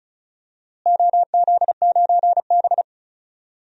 Morse code, heard as a keyed tone, about 36 wpm, 700 Hz. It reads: OZ9B
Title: DXCC-100 Call Signs - 4 Characters - Repeated 2x - 35wpm